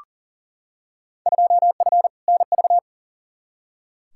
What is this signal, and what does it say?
Morse code, heard as a keyed tone, about 40 wpm, 700 Hz. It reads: 2F NV
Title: Field Day Exchanges - 40wpm